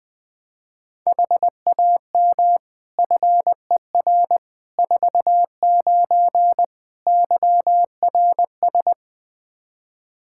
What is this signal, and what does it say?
Morse code, heard as a keyed tone, about 20 wpm, 700 Hz. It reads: HAM FER 49 YRS
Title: Practice QSO - Exchange 3 - More Info - 20wpm